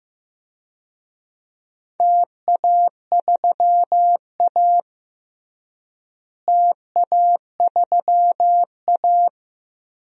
Morse code, heard as a keyed tone, about 15 wpm, 700 Hz. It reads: TA3A TA3A
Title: DXCC-100 Call Signs - 4 Characters - Repeated 2x - 15wpm